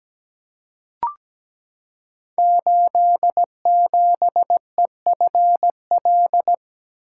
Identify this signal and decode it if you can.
Morse code, heard as a keyed tone, about 17 wpm, 700 Hz. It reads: 87EFL
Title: Code Groups - 5 Letter:Numbers - 17wpm